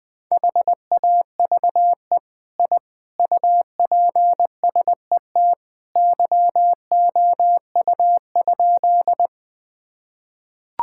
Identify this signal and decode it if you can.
Morse code, heard as a keyed tone, about 19 wpm, 700 Hz. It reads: HAVE I UPSET YOU?
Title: Sentences from Top 1000 Words - Review Part 5 - 20wpm